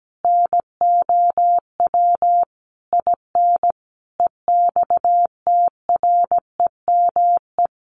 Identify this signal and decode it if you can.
Morse code, heard as a keyed tone, about 17 wpm, 700 Hz. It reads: NOW IN EXTREME
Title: Sentences from Top 1300 Words 17wpm